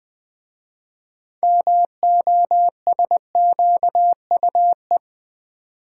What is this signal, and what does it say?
Morse code, heard as a keyed tone, about 20 wpm, 700 Hz. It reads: MOSQUE
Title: 6 Letter Words - First 3 Letters Given - 20wpm